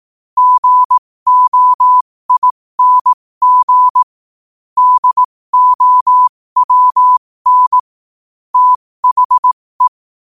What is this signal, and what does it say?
Morse code, heard as a keyed tone, about 18 wpm, spaced as for 15 wpm, 1000 Hz. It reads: GOING DOWN THE
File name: qotdfiles / 15wpm